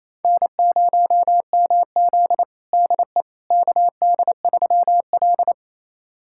Morse code, heard as a keyed tone, about 28 wpm, 700 Hz. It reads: N0MZ DE KD3L
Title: USA Callsign DE Callsign - 4 Characters - 28wpm